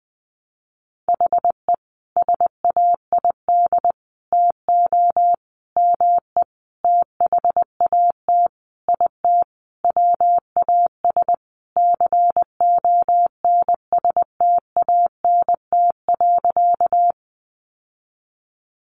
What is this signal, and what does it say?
Morse code, heard as a keyed tone, about 20 wpm, 700 Hz. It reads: HE SAID TO ME THAT IT WAS CONSTANT.
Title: Sentences from Top 1500 Words 20wpm